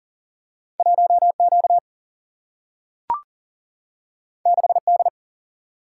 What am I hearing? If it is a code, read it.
Morse code, heard as a keyed tone, about 40 wpm, 700 Hz. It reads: 1Q 6D
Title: Single-Letter Single-Number 40wpm